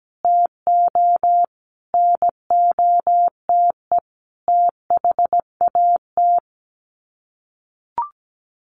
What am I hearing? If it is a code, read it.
Morse code, heard as a keyed tone, about 17 wpm, 700 Hz. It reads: TO NOTE THAT
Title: Sets of 5 Words Encore2 17wpm